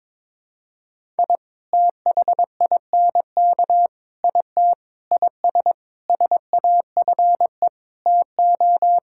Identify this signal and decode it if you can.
Morse code, heard as a keyed tone, about 22 wpm, 700 Hz. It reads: I THINK IT IS SAFE TO
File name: practice / 22wpm / Sets of 8 Words - 22wpm